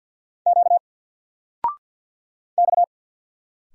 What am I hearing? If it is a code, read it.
Morse code, heard as a keyed tone, about 50 wpm, 700 Hz. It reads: Q X
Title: Single Letter-Number 50wpm